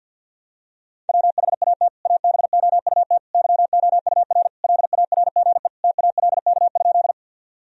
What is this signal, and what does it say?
Morse code, heard as a keyed tone, about 51 wpm, 700 Hz. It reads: WHAT ABOUT YOUR LARGE TALK?
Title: Sentences from Top 1000 Words - Review Part 1 - 50wpm